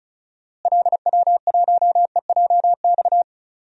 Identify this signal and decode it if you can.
Morse code, heard as a keyed tone, about 34 wpm, 700 Hz. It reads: LW1EJX
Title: DXCC-100 Call Signs - 6 Characters - Repeated 2x - 35wpm